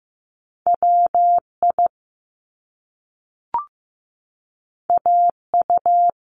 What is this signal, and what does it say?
Morse code, heard as a keyed tone, about 15 wpm, 700 Hz. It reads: WI AU
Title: Top Two Letter Combinations in Words 15wpm